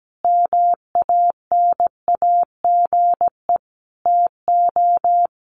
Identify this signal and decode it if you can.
Morse code, heard as a keyed tone, about 17 wpm, 700 Hz. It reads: MANAGE TO
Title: Sentences from Top 1400 Words 17wpm